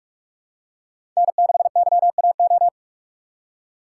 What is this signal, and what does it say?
Morse code, heard as a keyed tone, about 44 wpm, 700 Hz. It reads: N6YAO
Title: USA Call Signs - 5 Characters - 6x Character Spacing - 45wpm